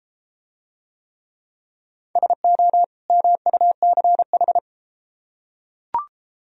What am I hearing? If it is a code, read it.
Morse code, heard as a keyed tone, about 32 wpm, 700 Hz. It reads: SO MUCH